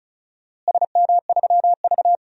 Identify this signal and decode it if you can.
Morse code, heard as a keyed tone, about 36 wpm, 700 Hz. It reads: SM3V
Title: DXCC-100 Call Signs - 4 Characters - Repeated 2x - 35wpm